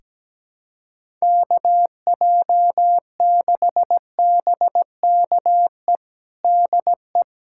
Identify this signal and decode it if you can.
Morse code, heard as a keyed tone, about 17 wpm, 700 Hz. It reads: KJ6BKE DE
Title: USA Callsign DE Callsign - 6 Characters - 17wpm